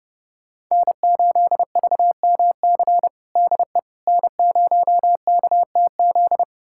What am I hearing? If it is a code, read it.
Morse code, heard as a keyed tone, about 30 wpm, 700 Hz. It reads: N8VMC DE N0KTZ
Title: USA Callsign DE Callsign - 5 Characters - 30wpm